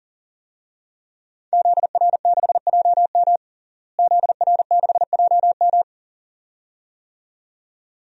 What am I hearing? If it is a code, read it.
Morse code, heard as a keyed tone, about 40 wpm, 700 Hz. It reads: ZR6JM ZR6JM
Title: DXCC-100 Call Signs - 5 Characters - Repeated 2x - 40wpm